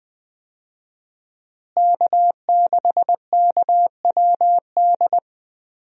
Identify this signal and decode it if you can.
Morse code, heard as a keyed tone, about 20 wpm, 700 Hz. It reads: K6KWD
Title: USA Call Signs - 5 Characters - Repeated 3x - 20wpm